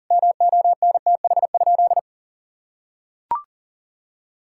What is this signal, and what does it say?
Morse code, heard as a keyed tone, about 40 wpm, 700 Hz. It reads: MONTH?
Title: Sentences from Top 1000 Words - Review Part 6 - 40wpm